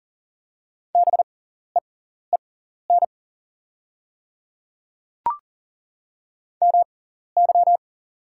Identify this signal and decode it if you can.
Morse code, heard as a keyed tone, about 40 wpm, spaced as for 14 wpm, 690 Hz. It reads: BEEN MY
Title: 6x Character Spacing - Top 300 Words - 40wpm